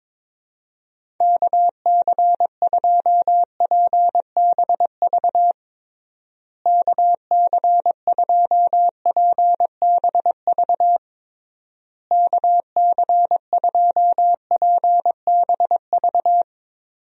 Morse code, heard as a keyed tone, about 22 wpm, 700 Hz. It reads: KC2PBV KC2PBV KC2PBV
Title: USA Call Signs - 6 Characters - Repeated 3x - 22wpm